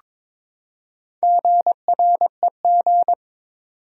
Morse code, heard as a keyed tone, about 22 wpm, 700 Hz. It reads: GREG
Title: USA Common Names - Max 4 Letters - 22wpm